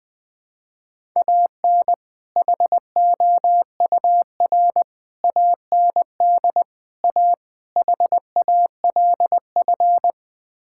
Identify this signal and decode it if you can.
Morse code, heard as a keyed tone, about 20 wpm, 700 Hz. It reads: AN HOUR AND A HALF
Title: Build A Phrase - 5-Word Phrases - 20wpm